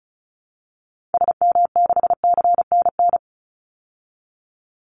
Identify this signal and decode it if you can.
Morse code, heard as a keyed tone, about 34 wpm, 700 Hz. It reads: SM6CNN
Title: Call Signs Repeated 3x - 35wpm